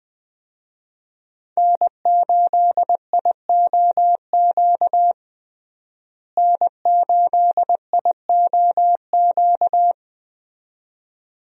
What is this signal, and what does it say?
Morse code, heard as a keyed tone, about 20 wpm, 700 Hz. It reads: N8IOQ N8IOQ
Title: USA Call Signs - 5 Characters - Repeated 2x - 20wpm